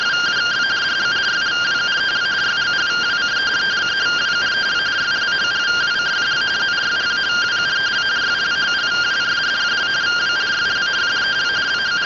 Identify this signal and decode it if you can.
Two-tone teleprinter signal, about 170 Hz shift.